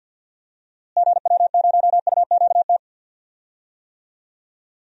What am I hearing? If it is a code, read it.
Morse code, heard as a keyed tone, about 49 wpm, 700 Hz. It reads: GW0UQT
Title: DXCC-100 Call Signs - 6 Characters - Repeated 2x - 50wpm